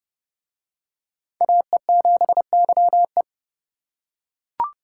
Morse code, heard as a keyed tone, about 27 wpm, 700 Hz. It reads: AE7YE